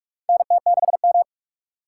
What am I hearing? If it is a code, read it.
Morse code, heard as a keyed tone, about 45 wpm, 700 Hz. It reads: NT6M
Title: USA Call Signs - 2x Character Spacing - 45wpm